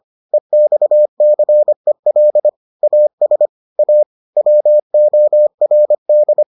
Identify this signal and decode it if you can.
Morse code, heard as a keyed tone, about 25 wpm, 600 Hz. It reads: EXCEL AS A WORD